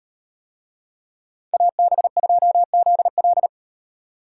Morse code, heard as a keyed tone, about 36 wpm, 700 Hz. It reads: AB2ZL